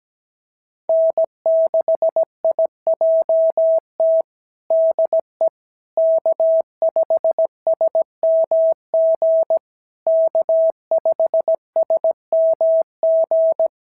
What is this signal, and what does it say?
Morse code, heard as a keyed tone, about 16 wpm, 650 Hz. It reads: N6IJT DE K5SMG K5SMG
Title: SOTA QSO Practice 17wpm